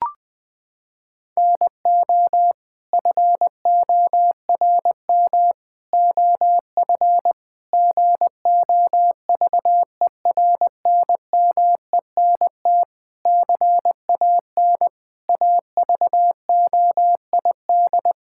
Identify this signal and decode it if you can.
Morse code, heard as a keyed tone, about 20 wpm, 700 Hz. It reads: NO FORM OF GOVERNMENT CAN AVOID
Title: Sentences from Top 1000 Words - Review Part 6 - 20wpm